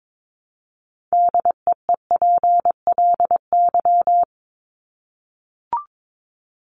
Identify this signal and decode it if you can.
Morse code, heard as a keyed tone, about 22 wpm, 700 Hz. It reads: DEEPLY
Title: Top 1751-2000 Words 22wpm